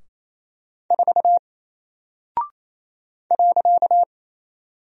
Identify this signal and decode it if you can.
Morse code, heard as a keyed tone, about 27 wpm, 700 Hz. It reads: 4 .
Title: Single Character (Letter:Number:Punctuation:Prosign) 28wpm